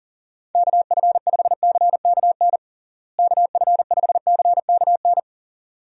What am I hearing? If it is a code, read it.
Morse code, heard as a keyed tone, about 37 wpm, 700 Hz. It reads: KF5CKN KF5CKN
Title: USA Call Signs - 6 Characters - Repeated 2x - 40wpm